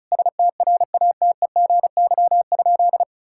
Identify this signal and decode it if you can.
Morse code, heard as a keyed tone, about 36 wpm, 690 Hz. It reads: STRATEGY?
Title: Sentences from Top 1000 Words - Review Part 5 - 35wpm